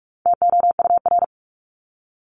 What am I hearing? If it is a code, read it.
Morse code, heard as a keyed tone, about 44 wpm, 700 Hz. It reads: TOUR
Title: Top 1251-1500 Words 45wpm